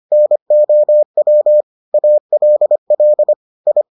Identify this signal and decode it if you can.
Morse code, heard as a keyed tone, about 25 wpm, 600 Hz. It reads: NOW ALL I